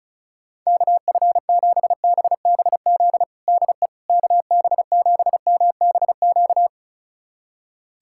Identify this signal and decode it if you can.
Morse code, heard as a keyed tone, about 34 wpm, 700 Hz. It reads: KF7BBZ DE KB7MBQ
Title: USA Callsign DE Callsign - 6 Characters - 35wpm